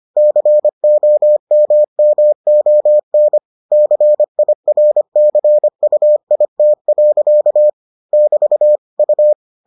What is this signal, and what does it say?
Morse code, heard as a keyed tone, about 25 wpm, 600 Hz. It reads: COMMON CIRCUIT. <BT> U